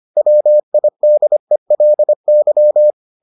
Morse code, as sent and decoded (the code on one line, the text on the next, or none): .-- .. -.. .-.. -.--
WIDELY